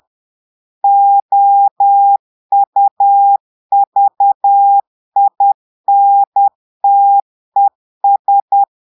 Morse code, heard as a keyed tone, about 10 wpm, 800 Hz. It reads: OUVINTES
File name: portugais texte en clair 2 vitesse 10 mots minutes